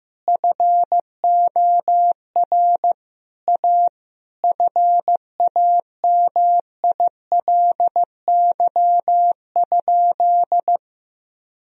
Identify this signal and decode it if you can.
Morse code, heard as a keyed tone, about 15 wpm, 700 Hz. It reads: FOR A FAMILY?
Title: Sentences from Top 1000 Words - Review Part 4 - 15wpm